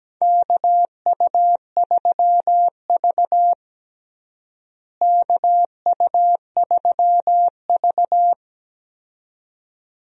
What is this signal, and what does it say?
Morse code, heard as a keyed tone, about 17 wpm, 700 Hz. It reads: KU3V KU3V
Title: USA Call Signs - 4 Characters - Repeated 2x - 17wpm